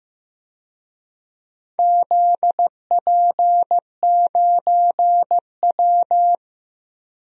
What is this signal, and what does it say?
Morse code, heard as a keyed tone, about 15 wpm, 700 Hz. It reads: ZP9W
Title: DXCC-100 Call Signs - 4 Characters - Repeated 2x - 15wpm